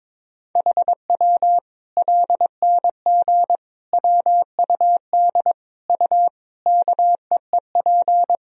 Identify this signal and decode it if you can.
Morse code, heard as a keyed tone, about 22 wpm, 700 Hz. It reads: HW LNG WUD U KEEP
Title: Sentences from Top 500 Words and Abbreviations 22wpm